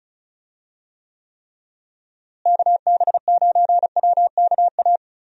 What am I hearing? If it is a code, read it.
Morse code, heard as a keyed tone, about 36 wpm, 700 Hz. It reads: KB9WKA
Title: USA Call Signs - 6 Characters - Repeated 3x - 35wpm